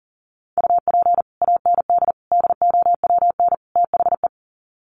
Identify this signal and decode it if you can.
Morse code, heard as a keyed tone, about 40 wpm, 700 Hz. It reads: UP AND DOWN THE
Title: Sets of 4 Words 40wpm